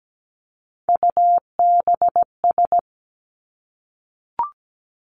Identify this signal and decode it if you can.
Morse code, heard as a keyed tone, about 17 wpm, 700 Hz. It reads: UBS
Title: Three Challenging Letter-Number Combinations 17wpm